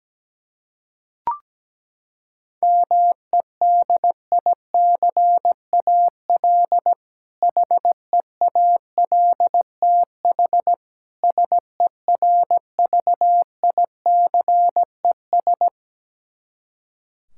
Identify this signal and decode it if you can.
Morse code, heard as a keyed tone, about 17 wpm, 700 Hz. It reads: MEDICAL HEALTH SERVICES
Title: K1USN SST QSO Practice 17wpm